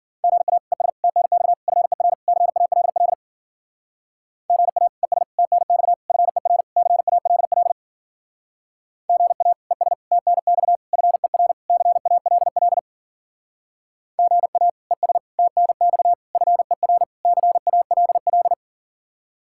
Morse code, heard as a keyed tone, about 48 wpm, 700 Hz. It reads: GA ES TNX FER CALL GA ES TNX FER CALL GA ES TNX FER CALL GA ES TNX FER CALL
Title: Speed-Racing Common QSO Phrases Encore - 40wpm